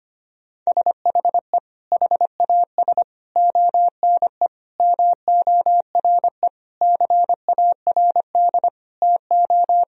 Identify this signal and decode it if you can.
Morse code, heard as a keyed tone, about 25 wpm, 700 Hz. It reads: SHE HAS ONE MORE CARD TO
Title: Sentences from Top 1000 Words - Review Part 4 - 25wpm